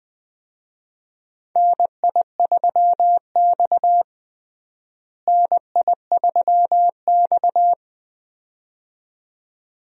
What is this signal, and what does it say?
Morse code, heard as a keyed tone, about 20 wpm, 700 Hz. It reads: NI3X NI3X
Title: USA Call Signs - 4 Characters - Repeated 2x - 20wpm